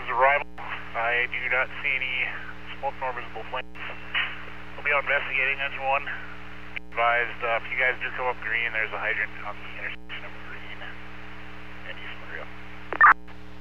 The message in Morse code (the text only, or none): none